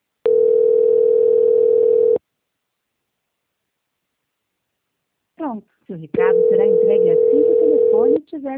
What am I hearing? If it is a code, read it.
Ringback tone (North America), 440 + 480 Hz, with speech.
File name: out- -2008- - - .359